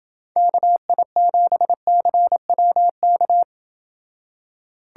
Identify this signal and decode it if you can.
Morse code, heard as a keyed tone, about 29 wpm, 700 Hz. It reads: KI7CWK